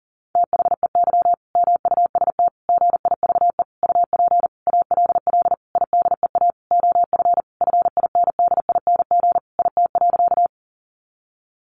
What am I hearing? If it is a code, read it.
Morse code, heard as a keyed tone, about 40 wpm, 700 Hz. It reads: THEY MUST GIVE UP ALL IDEA OF FINDING IT.
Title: Sentences from Top 1100 Words 40wpm